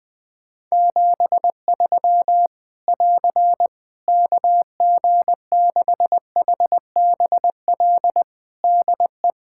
Morse code, heard as a keyed tone, about 20 wpm, 700 Hz. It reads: 73 <AR> KG6HBL DE
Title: Practice QSO - Exchange 4 - Ending - 20wpm